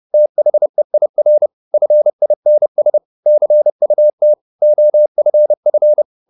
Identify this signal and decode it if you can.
Morse code, heard as a keyed tone, about 30 wpm, 600 Hz. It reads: THEIR FINS CUT OFF
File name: the old man and the sea / the old man and the sea 1 30wpm